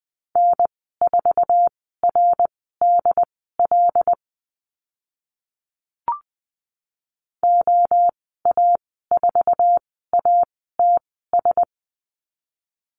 Morse code, heard as a keyed tone, about 20 wpm, spaced as for 14 wpm, 700 Hz. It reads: N4RDL OA4ATS
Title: 082 Lesson 14g - Number 2 - Call Signs 20wpm